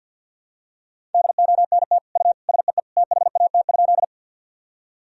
Morse code, heard as a keyed tone, about 49 wpm, 700 Hz. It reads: DONT U SEE THAT?
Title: Sentences from Top 300 Words and Abbreviations 50wpm